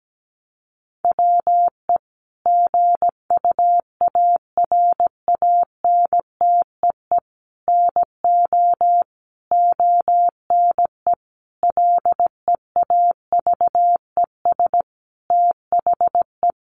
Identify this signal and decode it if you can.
Morse code, heard as a keyed tone, about 17 wpm, 700 Hz. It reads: WE GUARANTEE NO ONE LEAVES THE
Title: A Sound Of Thunder 17wpm